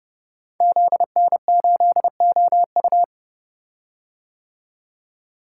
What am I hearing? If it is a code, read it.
Morse code, heard as a keyed tone, about 28 wpm, 700 Hz. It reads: ZN8OU
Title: Code Groups - 5 Letter:Numbers - 30wpm